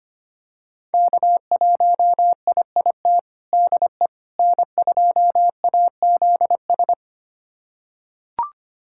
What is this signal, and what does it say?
Morse code, heard as a keyed tone, about 25 wpm, 700 Hz. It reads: K1IIT DE N2AZS
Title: USA Callsign DE Callsign - 5 Characters - 25wpm